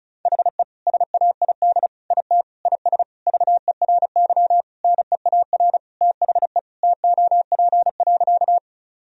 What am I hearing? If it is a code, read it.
Morse code, heard as a keyed tone, about 36 wpm, 700 Hz. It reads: HE SAID IT IS VERY NEAR THE TOP.
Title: Sentences from Top 1000 Words - Review Part 2 - 35wpm